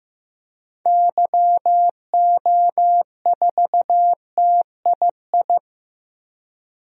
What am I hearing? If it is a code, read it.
Morse code, heard as a keyed tone, about 16 wpm, 700 Hz. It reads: YO4TII
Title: DXCC-100 Call Signs - 6 Characters - 15wpm